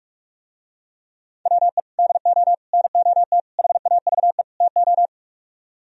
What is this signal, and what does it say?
Morse code, heard as a keyed tone, about 48 wpm, 700 Hz. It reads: WE DO NOT HAVE TO